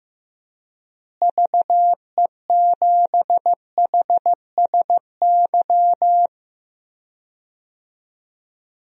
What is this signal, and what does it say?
Morse code, heard as a keyed tone, about 15 wpm, 700 Hz. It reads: VE7HSY
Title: DXCC-100 Call Signs - 6 Characters - Repeated 2x - 15wpm